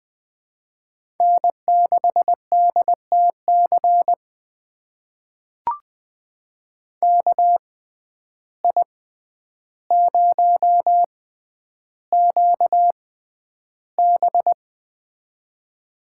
Morse code, heard as a keyed tone, about 21 wpm, spaced as for 10 wpm, 700 Hz. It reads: N6DTC KI0QB
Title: USA Call Signs - 5 Characters - 6x Character Spacing - 20wpm